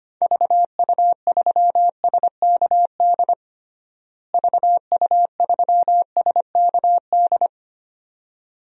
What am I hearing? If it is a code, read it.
Morse code, heard as a keyed tone, about 25 wpm, 700 Hz. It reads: VU3SKD VU3SKD
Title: DXCC-100 Call Signs - 6 Characters - Repeated 2x - 25wpm